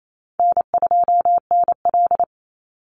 Morse code, heard as a keyed tone, about 28 wpm, 700 Hz. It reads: N2NL
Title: Easy Call Signs Repeated 3x - 28wpm